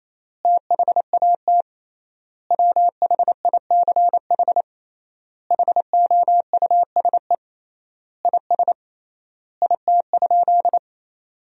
Morse code, heard as a keyed tone, about 28 wpm, 700 Hz. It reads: THAT WHICH HOUSE IS IT?
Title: 3x Word-Spacing - Sentences from Top 200 Words - 28wpm